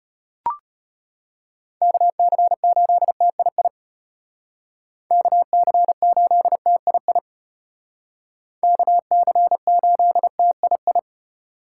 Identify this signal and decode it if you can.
Morse code, heard as a keyed tone, about 34 wpm, 700 Hz. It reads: KC8TII KC8TII KC8TII